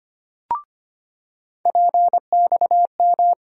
Morse code, heard as a keyed tone, about 25 wpm, 700 Hz. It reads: PXM
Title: USA Call Sign Postfixes - 3 Letters 25wpm